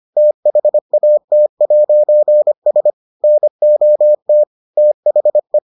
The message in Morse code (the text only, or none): THAT'S NOT THE